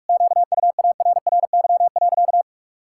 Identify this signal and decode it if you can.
Morse code, heard as a keyed tone, about 41 wpm, 700 Hz. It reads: QUARRY.
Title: Pangram Challenge v2 - 45wpm